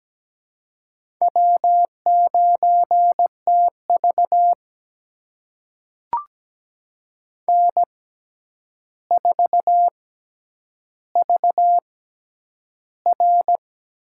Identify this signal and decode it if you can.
Morse code, heard as a keyed tone, about 17 wpm, spaced as for 9 wpm, 700 Hz. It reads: W9TV N4VR
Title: USA Call Signs - 4 Characters - 6x Character Spacing - 17wpm